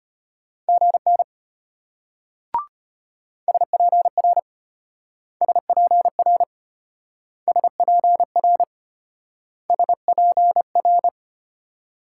Morse code, heard as a keyed tone, about 32 wpm, 700 Hz. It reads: GN SPR SPR SPR SPR